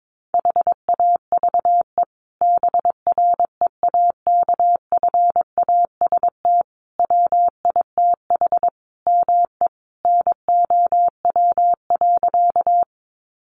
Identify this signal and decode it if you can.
Morse code, heard as a keyed tone, about 22 wpm, 700 Hz. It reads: HAVE BREAKFAST WITH ME NOW.
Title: Sentences from Top 1500 Words 22wpm